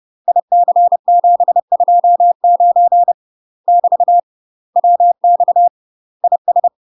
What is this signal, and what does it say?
Morse code, heard as a keyed tone, about 30 wpm, 700 Hz. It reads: IC729 <BT> WX IS
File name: UK MASTER QSOs TRAINING / 30WPM / UK MASTER QSOs 0027